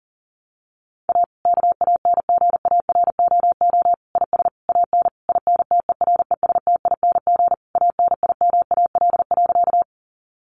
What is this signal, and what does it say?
Morse code, heard as a keyed tone, about 40 wpm, 700 Hz. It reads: A KANGAROO IS AN INTERESTING ANIMAL.
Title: Easy Phrases 40wpm